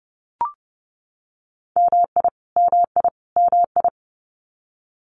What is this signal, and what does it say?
Morse code, heard as a keyed tone, about 30 wpm, 700 Hz. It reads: MI MI MI
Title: US States Repeated 3x 30wpm